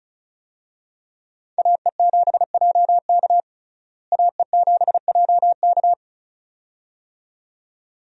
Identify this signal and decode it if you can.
Morse code, heard as a keyed tone, about 36 wpm, 700 Hz. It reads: AE7JK AE7JK
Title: USA Call Signs - 5 Characters - Repeated 2x - 35wpm